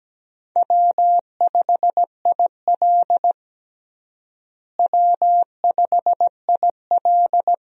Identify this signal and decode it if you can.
Morse code, heard as a keyed tone, about 17 wpm, 700 Hz. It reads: W5IL W5IL
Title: USA Call Signs - 4 Characters - Repeated 2x - 17wpm